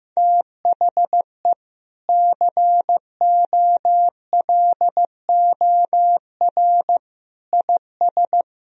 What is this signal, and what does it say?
Morse code, heard as a keyed tone, about 15 wpm, 700 Hz. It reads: THE COLOR IS
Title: Sentences from Top 1000 Words - Review Part 2 - 15wpm